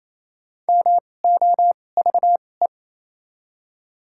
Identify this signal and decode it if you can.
Morse code, heard as a keyed tone, about 28 wpm, spaced as for 20 wpm, 700 Hz. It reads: MOVE
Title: 2x Character Spacing - Top 300 Words - 28wpm